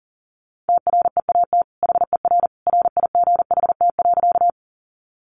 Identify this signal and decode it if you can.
Morse code, heard as a keyed tone, about 40 wpm, 700 Hz. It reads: TREAT HER RIGHT.
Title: Sentences from Top 1300 Words 40wpm